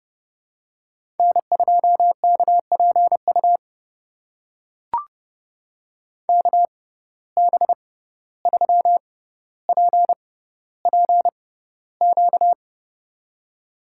Morse code, heard as a keyed tone, about 31 wpm, spaced as for 15 wpm, 700 Hz. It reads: N2KPU KB3PPQ
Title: USA Call Signs - 6x Character Spacing - 30wpm